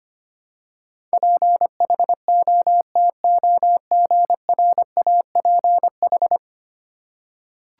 Morse code, heard as a keyed tone, about 25 wpm, 700 Hz. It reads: PHOTOGRAPH
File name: 10 Letter Words - First 3 Letters Given - 25wpm